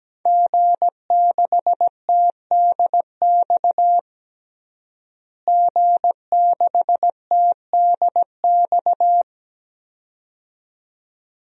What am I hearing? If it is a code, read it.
Morse code, heard as a keyed tone, about 17 wpm, 700 Hz. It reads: G6TDX G6TDX
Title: DXCC-100 Call Signs - 5 Characters - Repeated 2x - 17wpm